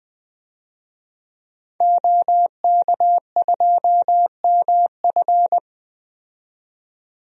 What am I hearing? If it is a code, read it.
Morse code, heard as a keyed tone, about 21 wpm, 700 Hz. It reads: OK2MF
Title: DXCC-100 Call Signs - 5 Characters - 20wpm